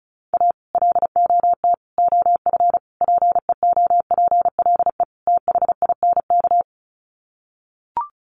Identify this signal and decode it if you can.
Morse code, heard as a keyed tone, about 34 wpm, 700 Hz. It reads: A LOT OF PEOPLE THINK
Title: Sets of 5 Words Encore 35wpm